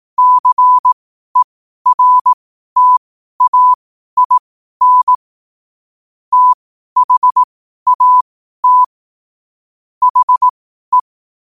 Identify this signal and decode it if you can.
Morse code, heard as a keyed tone, about 18 wpm, spaced as for 12 wpm, 1000 Hz. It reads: CERTAIN THAT HE
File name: qotdfiles / 12wpm